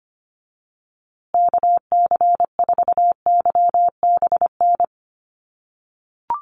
Morse code, heard as a keyed tone, about 25 wpm, 700 Hz. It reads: KC4YBN
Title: USA Call Signs Repeated 3x 25wpm